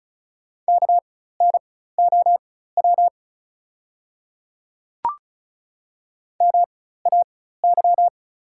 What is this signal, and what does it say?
Morse code, heard as a keyed tone, about 36 wpm, spaced as for 16 wpm, 690 Hz. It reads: KNOW MAY
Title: 4x Character Spacing - Top 100 Words - 35wpm